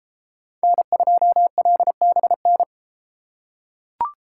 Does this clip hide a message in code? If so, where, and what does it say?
Yes: Morse code from 0.6 to 2.6 s, heard as a keyed tone, about 30 wpm, 700 Hz. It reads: N2LBN